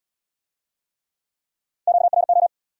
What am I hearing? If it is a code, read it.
Morse code, heard as a keyed tone, about 72 wpm, 700 Hz. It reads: QRO